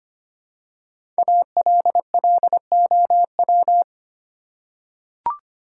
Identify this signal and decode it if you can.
Morse code, heard as a keyed tone, about 25 wpm, 700 Hz. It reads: ALLOW
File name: 5 Letter Words - First 3 Letters Given - 25wpm